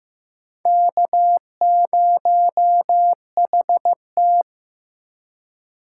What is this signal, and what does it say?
Morse code, heard as a keyed tone, about 15 wpm, 700 Hz. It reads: K0HT
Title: K1USN SST QSO Practice - 15wpm